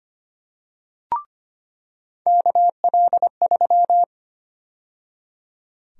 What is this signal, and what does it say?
Morse code, heard as a keyed tone, about 25 wpm, 700 Hz. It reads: KL3
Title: USA Call Sign Prefixes - 25wpm